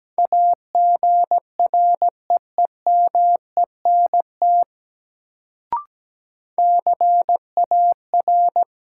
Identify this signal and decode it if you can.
Morse code, heard as a keyed tone, about 17 wpm, 700 Hz. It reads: AGREEMENT CAR
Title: Sets of 5 Words Encore - New Word Drill - 17wpm